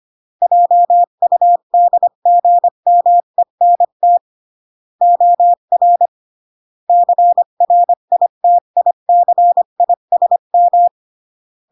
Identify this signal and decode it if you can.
Morse code, heard as a keyed tone, about 25 wpm, 700 Hz. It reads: JUDGMENT OR CRITICISM
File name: CW / jodorowsky / 82 commandments 25-20